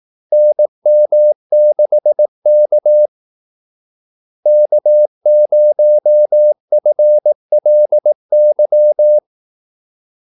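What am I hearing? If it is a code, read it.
Morse code, heard as a keyed tone, about 18 wpm, 600 Hz. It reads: NM6K K0FLY
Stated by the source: Morse call-sign practice at 18 wpm.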